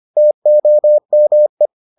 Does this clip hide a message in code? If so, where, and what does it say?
Yes: Morse code from 0.2 to 1.7 s, heard as a keyed tone, about 25 wpm, 600 Hz. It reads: TOME